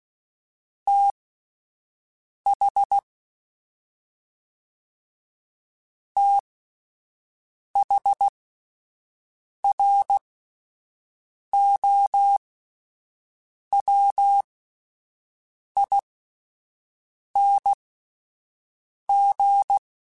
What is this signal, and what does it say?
Morse code, heard as a keyed tone, about 16 wpm, spaced as for 5 wpm, 800 Hz. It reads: TH THROWING
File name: THs / TH / TH Prefixes 1x1 @5